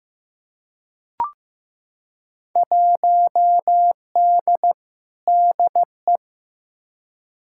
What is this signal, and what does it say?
Morse code, heard as a keyed tone, about 15 wpm, 700 Hz. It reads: 1D DE
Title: Field Day Exchanges - 15wpm